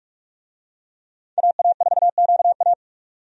Morse code, heard as a keyed tone, about 46 wpm, 700 Hz. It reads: AA4QA
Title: USA Call Signs - 5 Characters - 45wpm